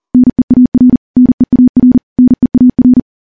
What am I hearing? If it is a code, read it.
Morse code, heard as a keyed tone, about 40 wpm, 250 Hz. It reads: NEAR NEAR NEAR
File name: morse / morse-training-40WPM